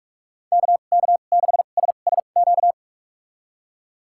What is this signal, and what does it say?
Morse code, heard as a keyed tone, about 46 wpm, spaced as for 33 wpm, 700 Hz. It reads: KK6SSQ
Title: USA Call Signs - 6 Characters - 2x Character Spacing - 45wpm